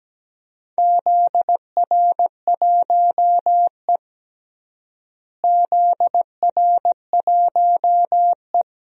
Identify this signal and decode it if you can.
Morse code, heard as a keyed tone, about 17 wpm, 700 Hz. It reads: ZR1E ZR1E
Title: DXCC-100 Call Signs - 4 Characters - Repeated 2x - 17wpm